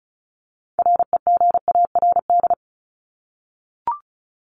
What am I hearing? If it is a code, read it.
Morse code, heard as a keyed tone, about 34 wpm, 700 Hz. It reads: REGARD
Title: ING Suffix - Top 100 Words with Base Words 35wpm